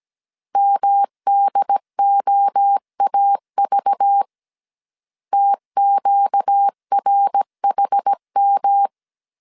Morse code, heard as a keyed tone, about 17 wpm, 800 Hz. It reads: MDOAV TQRHM
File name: 16wpm